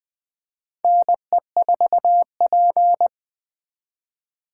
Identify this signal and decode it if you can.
Morse code, heard as a keyed tone, about 20 wpm, 700 Hz. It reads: NE4P
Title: USA Call Signs - 4 Characters - 4x Character Spacing - 20wpm